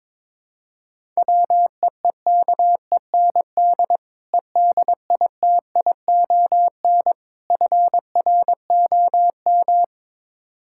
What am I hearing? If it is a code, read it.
Morse code, heard as a keyed tone, about 21 wpm, 700 Hz. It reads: WEEKEND EDITION FROM
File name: practice / 22wpm / Build A Phrase - 5-Word Phrases - 22wpm